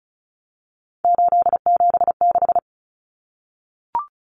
Morse code, heard as a keyed tone, about 34 wpm, 700 Hz. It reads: 876
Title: Common Cut Numbers - 3-Digit 35wpm